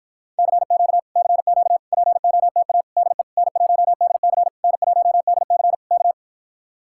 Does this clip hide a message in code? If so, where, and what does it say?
Yes: Morse code from 0.4 to 6.1 s, heard as a keyed tone, about 56 wpm, 700 Hz. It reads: CQ CQ POTA DE N1DC N1DC K